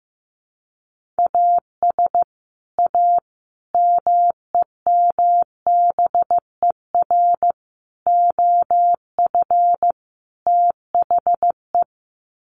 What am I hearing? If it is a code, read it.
Morse code, heard as a keyed tone, about 15 wpm, 700 Hz. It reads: AS A MEMBER OF THE
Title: Sets of 5 Words 15wpm